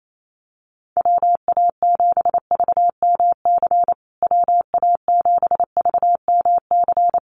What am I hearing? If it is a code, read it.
Morse code, heard as a keyed tone, about 28 wpm, 700 Hz. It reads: WA7VMC WA7VMC
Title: USA Call Signs Repeated 3x 28wpm